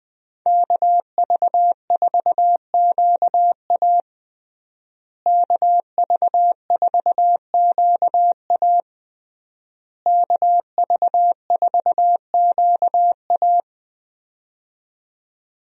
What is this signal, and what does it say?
Morse code, heard as a keyed tone, about 20 wpm, 700 Hz. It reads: KV4QA KV4QA KV4QA
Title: USA Call Signs - 5 Characters - Repeated 3x - 20wpm